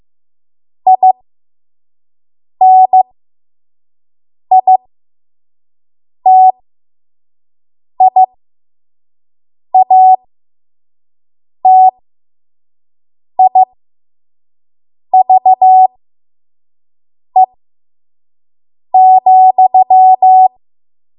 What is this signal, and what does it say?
Morse code, heard as a keyed tone, about 15 wpm, spaced as for 5 wpm, 750 Hz. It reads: INITIATIVE,
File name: Morse / 05WPM